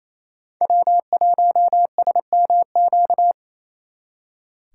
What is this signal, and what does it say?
Morse code, heard as a keyed tone, about 34 wpm, 700 Hz. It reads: W1SMQ
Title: Speed Racing - USA 1x3 Call Signs 28wpm